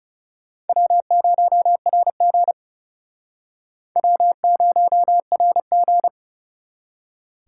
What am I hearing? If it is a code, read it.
Morse code, heard as a keyed tone, about 40 wpm, 700 Hz. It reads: W0RG W0RG